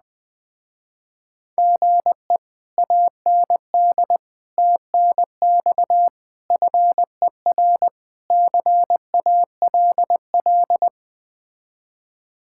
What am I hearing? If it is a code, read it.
Morse code, heard as a keyed tone, about 20 wpm, 700 Hz. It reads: GE AND TNX FER CALL
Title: Speed-Racing Common QSO Phrases - 17wpm